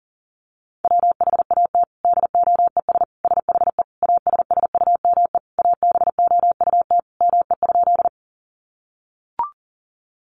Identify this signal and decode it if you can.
Morse code, heard as a keyed tone, about 40 wpm, 700 Hz. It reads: WHAT DOES SHE ASSUME ABOUT ME?
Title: Sentences from Top 1500 Words 40wpm